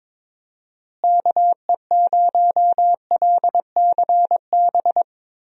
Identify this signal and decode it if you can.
Morse code, heard as a keyed tone, about 21 wpm, 700 Hz. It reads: KE0LCB
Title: USA Call Signs - 6 Characters - Repeated 3x - 22wpm